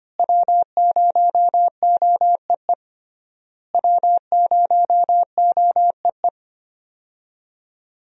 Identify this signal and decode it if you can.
Morse code, heard as a keyed tone, about 25 wpm, 700 Hz. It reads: W0OEE W0OEE
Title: USA Call Signs - 5 Characters - Repeated 3x - 25wpm